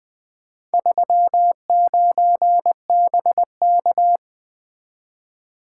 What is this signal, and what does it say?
Morse code, heard as a keyed tone, about 20 wpm, 700 Hz. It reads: 39BK
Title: Four Letter-Number Combinations - 20wpm